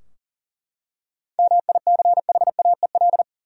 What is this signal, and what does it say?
Morse code, heard as a keyed tone, about 40 wpm, 700 Hz. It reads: MICHAEL
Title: USA Common Names - 40wpm